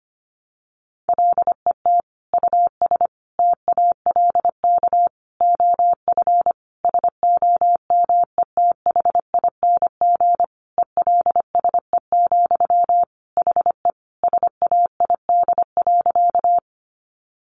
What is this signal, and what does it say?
Morse code, heard as a keyed tone, about 25 wpm, 700 Hz. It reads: LET US TALK OF SOMETHING ELSE, HE SAID.
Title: Sentences from Top 800 Words 25wpm